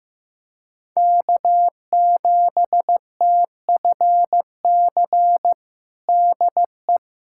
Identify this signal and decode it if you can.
Morse code, heard as a keyed tone, about 15 wpm, 700 Hz. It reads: K7TFC DE
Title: USA Callsign DE Callsign 15wpm